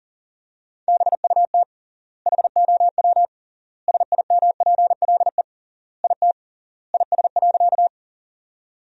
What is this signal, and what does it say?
Morse code, heard as a keyed tone, about 37 wpm, 700 Hz. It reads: BUT HOW SIMPLE IT IS.
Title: 3x Word-Spacing - Sentences from Top 400 Words - 40wpm